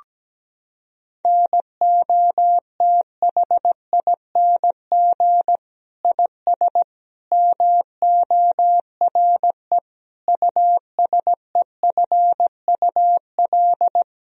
Morse code, heard as a keyed tone, about 17 wpm, 700 Hz. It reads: NOTHING IS MORE USEFUL
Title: Sentences from Top 1000 Words - Review Part 4 - 17wpm